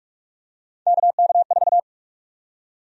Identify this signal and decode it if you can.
Morse code, heard as a keyed tone, about 43 wpm, 700 Hz. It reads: KK4
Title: USA Call Sign Prefixes - 45wpm